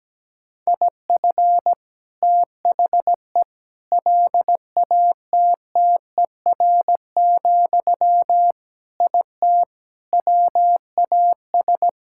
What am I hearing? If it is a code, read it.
Morse code, heard as a keyed tone, about 17 wpm, 700 Hz. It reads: IF THE LATTER, IT WAS
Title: Sentences from Top 1600 Words 17wpm